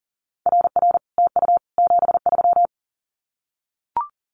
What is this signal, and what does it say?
Morse code, heard as a keyed tone, about 40 wpm, 700 Hz. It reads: RR TU 73
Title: Common QSO Phrases Encore 40wpm